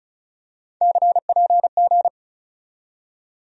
Morse code, heard as a keyed tone, about 34 wpm, 700 Hz. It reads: CPG
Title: USA Call Sign Postfixes - 3 Letters 35wpm